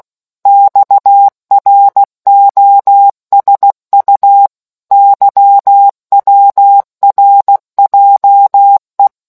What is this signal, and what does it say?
Morse code, heard as a keyed tone, about 16 wpm, 800 Hz. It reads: XROSU YWRJE